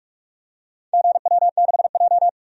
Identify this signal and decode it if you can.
Morse code, heard as a keyed tone, about 46 wpm, 700 Hz. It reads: GW6J
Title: DXCC-100 Call Signs - 4 Characters - 45wpm